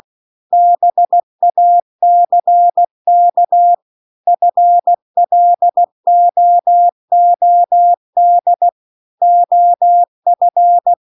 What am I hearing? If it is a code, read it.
Morse code, heard as a keyed tone, about 16 wpm, 700 Hz. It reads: BACK FLOOD OF
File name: CW-The-Big-Twoheaded-River-2-16WPM